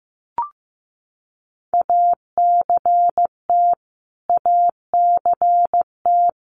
Morse code, heard as a keyed tone, about 15 wpm, 700 Hz. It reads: ACT ACT
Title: Top 300 Words Repeated 15wpm